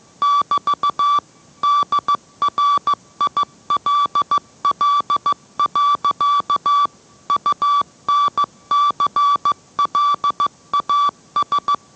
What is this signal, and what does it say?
Morse code, heard as a keyed tone, about 15 wpm, 1150 Hz. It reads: <BT> DRILL. UNCLAS